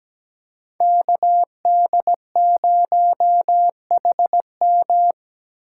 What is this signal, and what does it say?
Morse code, heard as a keyed tone, about 17 wpm, 700 Hz. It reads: KD0HM
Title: USA Call Signs - 3x Character Spacing - 17wpm